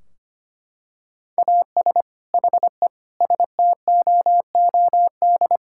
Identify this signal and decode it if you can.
Morse code, heard as a keyed tone, about 25 wpm, 700 Hz. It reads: AS HE STOOD
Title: Sentences from Top 1600 Words 25wpm